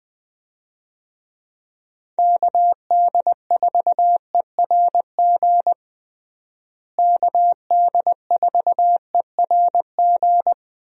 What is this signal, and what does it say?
Morse code, heard as a keyed tone, about 20 wpm, 700 Hz. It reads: KD4ERG KD4ERG
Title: USA Call Signs - 6 Characters - Repeated 2x - 20wpm